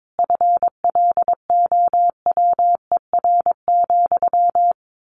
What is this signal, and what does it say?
Morse code, heard as a keyed tone, about 22 wpm, 700 Hz. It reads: FLOWER,
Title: A Sound Of Thunder 22wpm